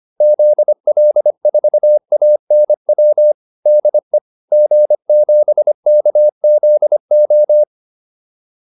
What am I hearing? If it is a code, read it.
Morse code, heard as a keyed tone, about 25 wpm, 600 Hz. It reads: ZL4ANW DE G7KZO